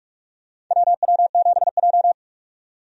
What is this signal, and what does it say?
Morse code, heard as a keyed tone, about 47 wpm, 700 Hz. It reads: WW7J
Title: USA Call Signs - 4 Characters - 45wpm